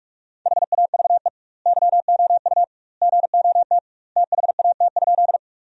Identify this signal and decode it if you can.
Morse code, heard as a keyed tone, about 41 wpm, 700 Hz. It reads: HAVE YOU GOT THAT?
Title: 2x Word-Spacing - Sentences from Top 300 Words - 45wpm